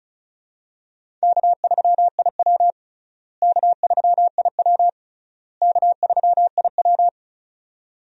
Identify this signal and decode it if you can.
Morse code, heard as a keyed tone, about 36 wpm, 700 Hz. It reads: K3IW K3IW K3IW
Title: USA Call Signs - 4 Characters - Repeated 3x - 35wpm